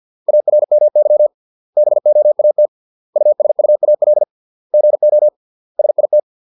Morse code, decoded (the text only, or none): ARMY BOAT USUAL GO SIT